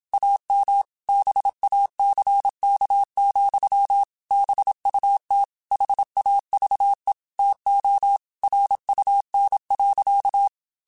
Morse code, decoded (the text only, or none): AM BACK, BUT HAVE TO RUN.